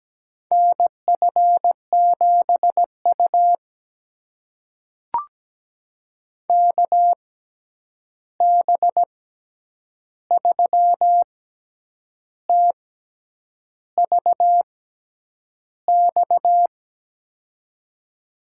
Morse code, heard as a keyed tone, about 17 wpm, spaced as for 8 wpm, 700 Hz. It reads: NF7U KB3TVX
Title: USA Call Signs - 6x Character Spacing - 17wpm